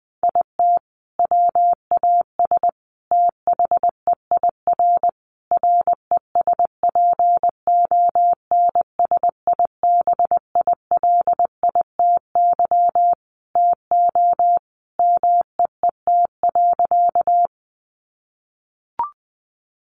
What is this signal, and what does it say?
Morse code, heard as a keyed tone, about 20 wpm, 700 Hz. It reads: IT WAS THEIR RESPONSIBILITY TO MEET.
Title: Sentences from Top 1200 Words 20wpm